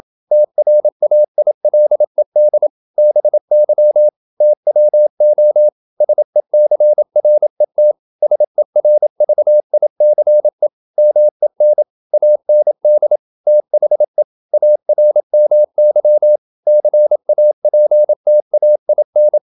Morse code, heard as a keyed tone, about 27 wpm, 600 Hz. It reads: TRAILED BY TWO SECRET SERVICE MEN AND THE ARMY CAPTAIN
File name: Never 0027wpm